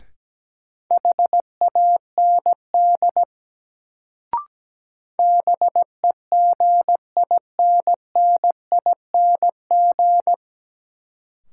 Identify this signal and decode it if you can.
Morse code, heard as a keyed tone, about 17 wpm, 700 Hz. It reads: HAND BEGINNING
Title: Sets of 3 Words - New Word Drill - 17wpm